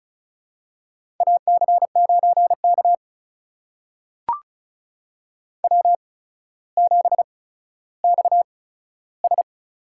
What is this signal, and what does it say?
Morse code, heard as a keyed tone, about 36 wpm, spaced as for 15 wpm, 700 Hz. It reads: AC9K W7XS
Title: USA Call Signs - 4 Characters - 8x Character Spacing - 35wpm